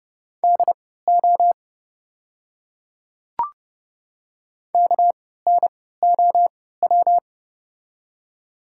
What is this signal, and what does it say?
Morse code, heard as a keyed tone, about 30 wpm, spaced as for 17 wpm, 700 Hz. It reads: DO KNOW
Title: 3x Character Spacing - Top 100 Words - 30wpm